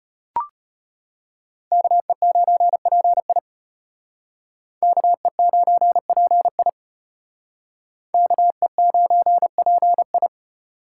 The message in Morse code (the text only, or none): KE9PI KE9PI KE9PI